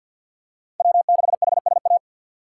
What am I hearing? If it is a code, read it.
Morse code, heard as a keyed tone, about 48 wpm, 700 Hz. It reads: W6HSA
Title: USA Call Signs - 5 Characters - 3x Character Spacing - 50wpm